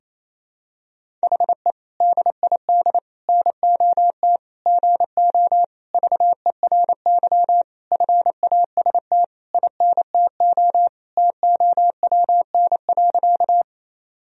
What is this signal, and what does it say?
Morse code, heard as a keyed tone, about 28 wpm, 700 Hz. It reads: HE DID NOT GO VERY FAST INTO TOWN.
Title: Sentences from Top 1000 Words - Review Part 2 - 28wpm